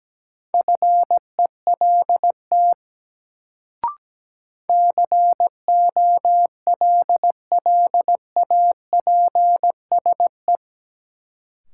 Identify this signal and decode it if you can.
Morse code, heard as a keyed tone, about 17 wpm, 700 Hz. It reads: FELT COLLAPSE
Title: Sets of 5 Words Encore - New Word Drill - 17wpm